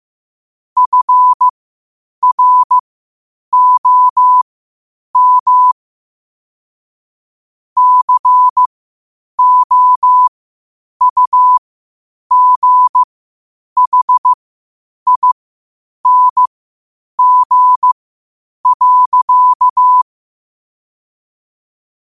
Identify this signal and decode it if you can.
Morse code, heard as a keyed tone, about 15 wpm, spaced as for 8 wpm, 1000 Hz. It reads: FROM COUGHING.
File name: qotdfiles / 8wpm